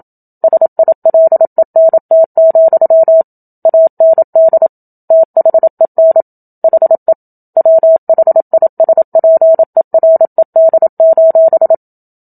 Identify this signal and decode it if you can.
Morse code, heard as a keyed tone, about 27 wpm, 650 Hz. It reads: SILENT, AND THEN HE WHISPERED:
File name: Book Two / 01-Chapter 1 - 27 WPM